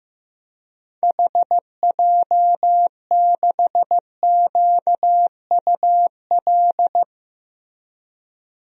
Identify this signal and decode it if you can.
Morse code, heard as a keyed tone, about 15 wpm, 700 Hz. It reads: HJ6QUL
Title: DXCC-100 Call Signs - 6 Characters - Repeated 3x - 15wpm